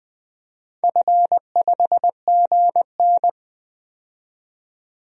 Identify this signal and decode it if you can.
Morse code, heard as a keyed tone, about 20 wpm, 700 Hz. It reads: F5GN
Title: DXCC-100 Call Signs - 4 Characters - 20wpm